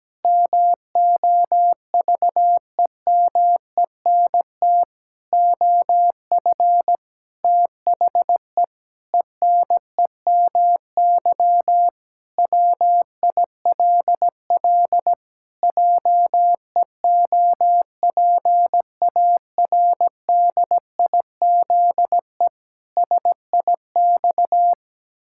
Morse code, heard as a keyed tone, about 17 wpm, 700 Hz. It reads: MOVEMENT OF THE ENEMY WILL JEOPARDIZE SIX
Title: Pangram Challenge v2 - 17wpm